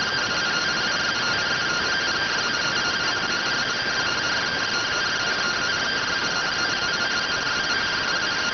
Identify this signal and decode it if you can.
Two-tone teleprinter signal, about 170 Hz shift.